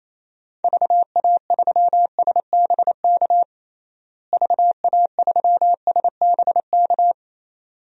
Morse code, heard as a keyed tone, about 27 wpm, 700 Hz. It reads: VA3SBK VA3SBK
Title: DXCC-100 Call Signs - 6 Characters - Repeated 2x - 28wpm